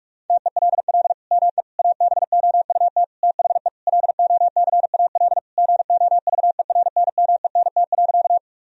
Morse code, heard as a keyed tone, about 46 wpm, 700 Hz. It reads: TELL ME ABOUT THE LOCAL GOVERNMENT.
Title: Sentences from Top 1000 Words - Review Part 3 - 45wpm